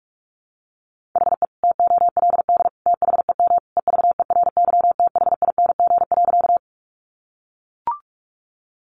Morse code, heard as a keyed tone, about 44 wpm, 700 Hz. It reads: HE TOLD THEM EVERYTHING.
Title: Sentences from Top 700 Words 45wpm